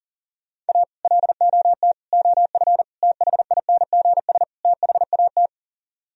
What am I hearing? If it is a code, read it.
Morse code, heard as a keyed tone, about 51 wpm, 700 Hz. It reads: A LOT OF THINGS THAT